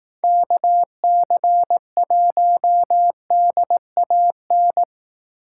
Morse code, heard as a keyed tone, about 18 wpm, 700 Hz. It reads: KC1DAN
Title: Speed Racing - USA Call Signs 15wpm